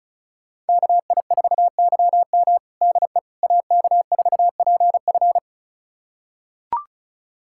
Morse code, heard as a keyed tone, about 36 wpm, 700 Hz. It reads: KI4YM DE AK4PF
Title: USA Callsign DE Callsign - 5 Characters - 35wpm